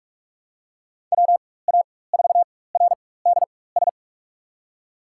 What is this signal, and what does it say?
Morse code, heard as a keyed tone, about 47 wpm, spaced as for 21 wpm, 700 Hz. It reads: WA4RDS
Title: USA Call Signs - 6 Characters - 4x Character Spacing - 45wpm